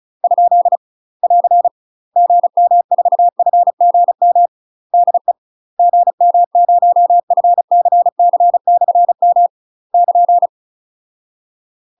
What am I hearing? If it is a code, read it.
Morse code, heard as a keyed tone, about 33 wpm, 700 Hz. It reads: ? <AR> GM4FGM DE GM0FCC/M <KN>
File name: UK MASTER QSOs TRAINING / 35WPM / UK MASTER QSOs 0007